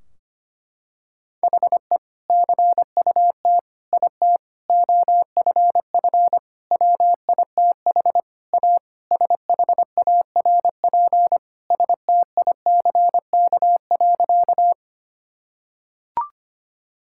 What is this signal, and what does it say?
Morse code, heard as a keyed tone, about 25 wpm, 700 Hz. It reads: HE CUT IT OFF WITH A SHARP STICK.
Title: Sentences from Top 1600 Words 25wpm